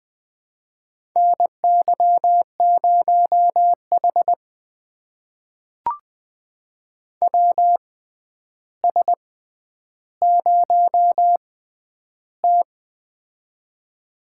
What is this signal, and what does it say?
Morse code, heard as a keyed tone, about 20 wpm, spaced as for 10 wpm, 700 Hz. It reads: NY0H WS0T
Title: USA Call Signs - 4 Characters - 6x Character Spacing - 20wpm